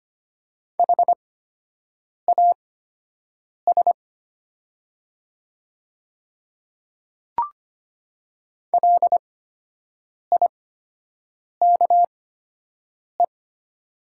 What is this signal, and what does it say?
Morse code, heard as a keyed tone, about 25 wpm, spaced as for 7 wpm, 690 Hz. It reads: HAS LIKE
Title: 8x Character Spacing - Top 200 Words - 25wpm